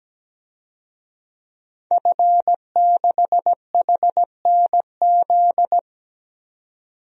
Morse code, heard as a keyed tone, about 17 wpm, 700 Hz. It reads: F6HNZ
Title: DXCC-100 Call Signs - 5 Characters - 17wpm